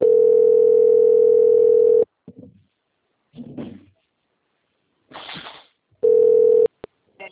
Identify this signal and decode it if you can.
Ringback tone (North America), 440 + 480 Hz.
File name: out- - - - - .9782